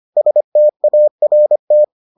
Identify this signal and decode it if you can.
Morse code, heard as a keyed tone, about 25 wpm, 600 Hz. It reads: START